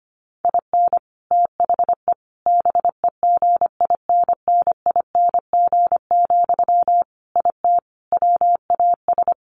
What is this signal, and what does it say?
Morse code, heard as a keyed tone, about 25 wpm, 700 Hz. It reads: IN THE BEGINNING, IT WAS
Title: Sentences from Top 1400 Words 25wpm